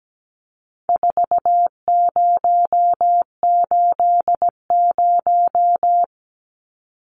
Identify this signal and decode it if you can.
Morse code, heard as a keyed tone, about 17 wpm, 700 Hz. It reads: 4080
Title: Numbers - 4-Digit 17wpm